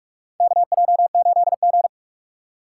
Morse code, heard as a keyed tone, about 46 wpm, 700 Hz. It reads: KJ8G
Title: USA Call Signs - 2x Character Spacing - 45wpm